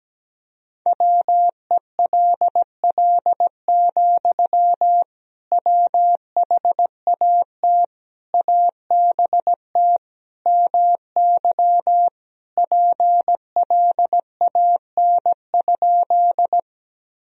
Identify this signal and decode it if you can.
Morse code, heard as a keyed tone, about 17 wpm, 700 Hz. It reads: WELL, WHAT ABT MY PLAN?
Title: Sentences from Top 500 Words and Abbreviations 17wpm